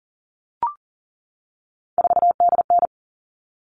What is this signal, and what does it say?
Morse code, heard as a keyed tone, about 40 wpm, 700 Hz. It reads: 4DN
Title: All Cut Numbers - 3-Digit 40wpm